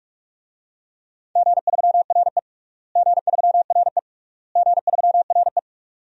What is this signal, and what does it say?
Morse code, heard as a keyed tone, about 46 wpm, 700 Hz. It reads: G3RE G3RE G3RE
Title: DXCC-100 Call Signs - 4 Characters - Repeated 3x - 45wpm